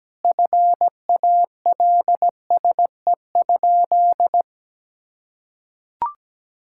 Morse code, heard as a keyed tone, about 17 wpm, 700 Hz. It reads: FALSE?
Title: Sentences from Top 1600 Words 17wpm